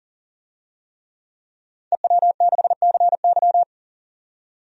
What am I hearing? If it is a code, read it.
Morse code, heard as a keyed tone, about 40 wpm, 700 Hz. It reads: EW6CY
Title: DXCC-100 Call Signs - 5 Characters - 40wpm